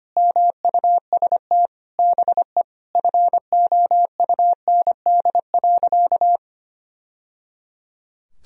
Morse code, heard as a keyed tone, about 25 wpm, 700 Hz. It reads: MUST BE FOUND.
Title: Sentences from Top 1000 Words - Review Part 4 - 25wpm